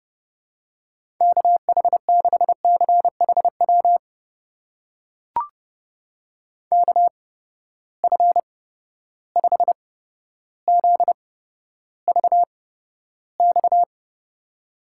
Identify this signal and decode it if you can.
Morse code, heard as a keyed tone, about 30 wpm, spaced as for 13 wpm, 700 Hz. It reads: KH6CHW KF5ZVX
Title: USA Call Signs - 6 Characters - 8x Character Spacing - 30wpm